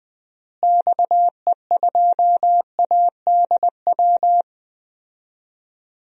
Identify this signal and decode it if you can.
Morse code, heard as a keyed tone, about 20 wpm, 700 Hz. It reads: XE2ADW
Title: DXCC-100 Call Signs - 6 Characters - Repeated 3x - 20wpm